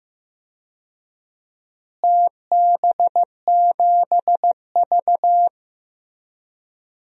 Morse code, heard as a keyed tone, about 15 wpm, 700 Hz. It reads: TB7V
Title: DXCC-100 Call Signs - 4 Characters - Repeated 3x - 15wpm